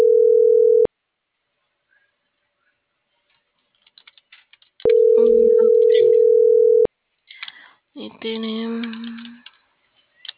Ringback tone (North America), 440 + 480 Hz.